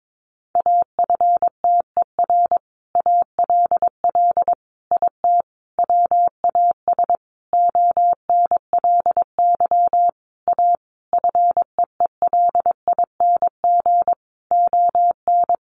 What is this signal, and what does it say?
Morse code, heard as a keyed tone, about 22 wpm, 700 Hz. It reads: AFTER ALL IT WAS ONLY A FEELING ON
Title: Sentences from Top 1200 Words 22wpm